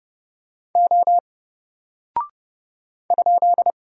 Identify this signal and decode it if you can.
Morse code, heard as a keyed tone, about 30 wpm, 700 Hz. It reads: O ?
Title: Single Character (Letter:Number:Punctuation:Prosign) Alt 30wpm